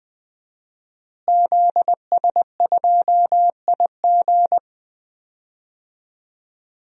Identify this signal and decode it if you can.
Morse code, heard as a keyed tone, about 20 wpm, 700 Hz. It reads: ZS2IG
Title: DXCC-100 Call Signs - 5 Characters - Repeated 3x - 20wpm